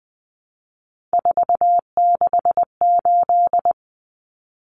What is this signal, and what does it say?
Morse code, heard as a keyed tone, about 20 wpm, 700 Hz. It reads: 468
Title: Numbers - 3-Digit 20wpm